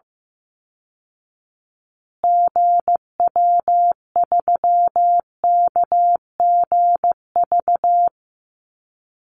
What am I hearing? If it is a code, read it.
Morse code, heard as a keyed tone, about 15 wpm, 700 Hz. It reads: GW3KGV
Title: Call Signs Repeated 3x - 15wpm